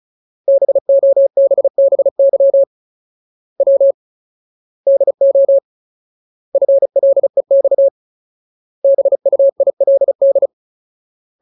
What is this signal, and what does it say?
Morse code, heard as a keyed tone, about 34 wpm, 550 Hz. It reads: BOBBY W DO FLEX BUILD
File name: Ultimate Word List / 35WPM Ultimate